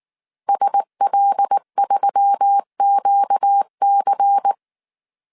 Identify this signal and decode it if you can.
Morse code, heard as a keyed tone, about 19 wpm, 800 Hz. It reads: SL3QC
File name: CW / DET / 1 6 1500C S C 18wpm 2356